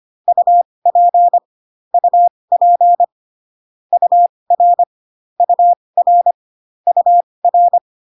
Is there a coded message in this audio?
Yes: Morse code from 0.3 to 7.8 s, heard as a keyed tone, about 25 wpm, spaced as for 20 wpm, 700 Hz. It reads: UP UP UR UR UR